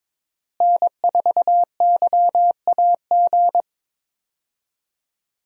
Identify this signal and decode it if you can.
Morse code, heard as a keyed tone, about 21 wpm, 700 Hz. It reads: N4YAG
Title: USA Call Signs - 5 Characters - Repeated 3x - 22wpm